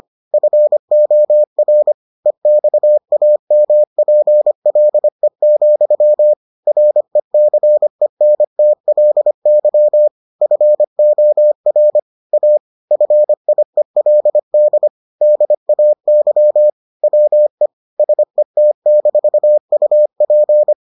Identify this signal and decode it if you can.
Morse code, heard as a keyed tone, about 25 wpm, 600 Hz. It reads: FOR EXAMPLE, RECENTLY FOR A FIELD DAY WE SET-UP